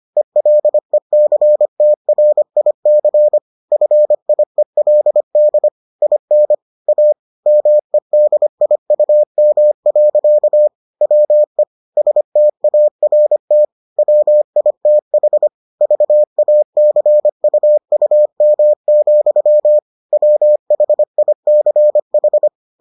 Morse code, heard as a keyed tone, about 25 wpm, 600 Hz. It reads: ELECTRIC FIELD IN A MEDIUM. WE START WITH VACUUM, WHICH